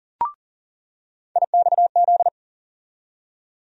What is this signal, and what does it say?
Morse code, heard as a keyed tone, about 40 wpm, 700 Hz. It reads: IXZ
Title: USA Call Sign Postfixes - 3 Letters 40wpm